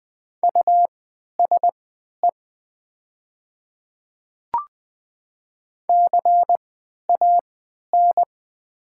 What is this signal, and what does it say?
Morse code, heard as a keyed tone, about 20 wpm, spaced as for 11 wpm, 700 Hz. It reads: USE CAN
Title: 3x Character Spacing - Top 500 Words - 20wpm